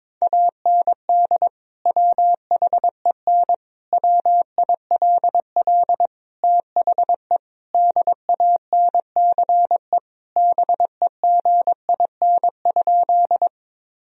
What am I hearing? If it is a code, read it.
Morse code, heard as a keyed tone, about 22 wpm, 700 Hz. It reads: AND WHEN WILL THE DANCE BEGIN?
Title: Sentences from Top 1000 Words - Review Part 4 - 22wpm